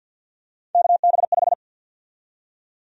Morse code, heard as a keyed tone, about 51 wpm, 700 Hz. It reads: KB5
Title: USA Call Sign Prefixes - 50wpm